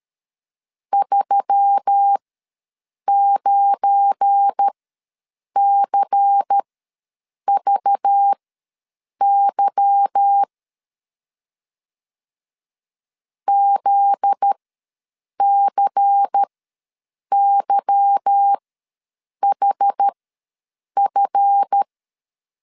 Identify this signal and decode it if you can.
Morse code, heard as a keyed tone, about 13 wpm, spaced as for 7 wpm, 800 Hz. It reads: 39CVY ZCYHF
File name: LEZIONI / 8 YJ45 / 8 Y ws 12wpm 3409